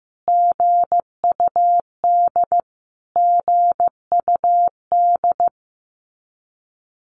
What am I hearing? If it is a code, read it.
Morse code, heard as a keyed tone, about 15 wpm, 700 Hz. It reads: GUD GUD
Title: Abbreviations Repeated 3x - 15wpm